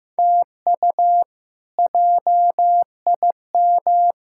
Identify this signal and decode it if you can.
Morse code, heard as a keyed tone, about 15 wpm, 700 Hz. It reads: TU JIM
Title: K1USN SST QSO Practice - 15wpm